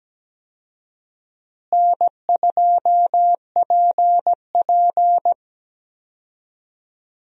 Morse code, heard as a keyed tone, about 17 wpm, 700 Hz. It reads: N2PP
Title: USA Call Signs - 4 Characters - Repeated 2x - 17wpm